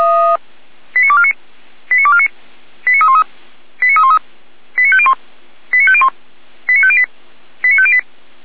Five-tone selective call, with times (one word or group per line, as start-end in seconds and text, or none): 1.0-1.3 89249
1.9-2.3 89249
2.9-3.2 89313
3.8-4.2 89313
4.8-5.1 89501
5.7-6.1 89501
6.7-7.0 89508
7.6-8.0 89508